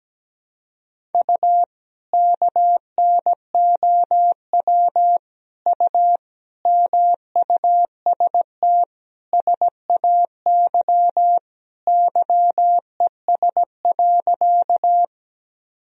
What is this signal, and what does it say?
Morse code, heard as a keyed tone, about 17 wpm, 700 Hz. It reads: U KNOW U MUST SAY YES.
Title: Sentences from Top 500 Words and Abbreviations 17wpm